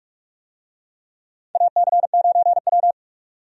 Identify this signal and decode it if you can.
Morse code, heard as a keyed tone, about 46 wpm, 700 Hz. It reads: AC9W
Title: USA Call Signs - 4 Characters - 45wpm